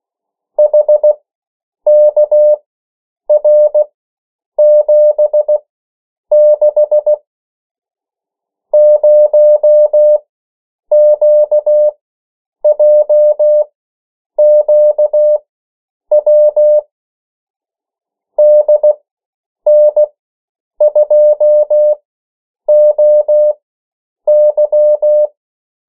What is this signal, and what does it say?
Morse code, heard as a keyed tone, about 16 wpm, spaced as for 8 wpm, 600 Hz. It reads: HKR76 0QJQW DN2OY
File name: morse mp3 / 14 urok 13